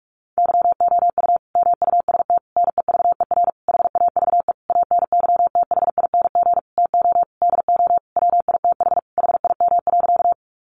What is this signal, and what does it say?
Morse code, heard as a keyed tone, about 44 wpm, 700 Hz. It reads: YOU MUST NEVER HAVE ANYTHING TO DO WITH HIM.
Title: Sentences from Top 1000 Words 45wpm